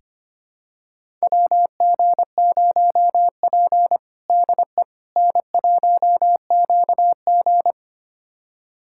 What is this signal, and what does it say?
Morse code, heard as a keyed tone, about 25 wpm, 700 Hz. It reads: WG0P DE N1QG
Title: USA Callsign DE Callsign - 4 Characters - 25wpm